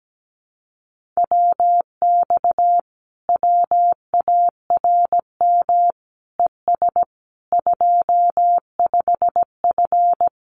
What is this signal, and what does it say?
Morse code, heard as a keyed tone, about 17 wpm, 700 Hz. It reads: WX WARM ES 25F
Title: Common QSO Phrases Encore 17wpm